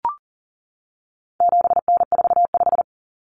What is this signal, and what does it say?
Morse code, heard as a keyed tone, about 40 wpm, 700 Hz. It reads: 7N45
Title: Common Cut Numbers - 4-Digit 40wpm